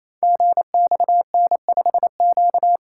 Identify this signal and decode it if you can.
Morse code, heard as a keyed tone, about 27 wpm, 700 Hz. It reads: GXN5Q
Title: Code Groups - 5 Letter:Numbers - 28wpm